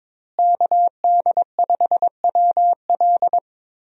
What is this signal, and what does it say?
Morse code, heard as a keyed tone, about 22 wpm, 700 Hz. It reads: KD5WL
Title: USA Call Signs - 5 Characters - Repeated 2x - 22wpm